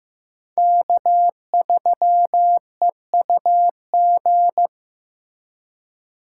Morse code, heard as a keyed tone, about 15 wpm, 700 Hz. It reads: K3EUG
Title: USA Call Signs - 5 Characters - 3x Character Spacing - 15wpm